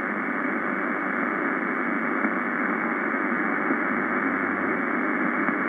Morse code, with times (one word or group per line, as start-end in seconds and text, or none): none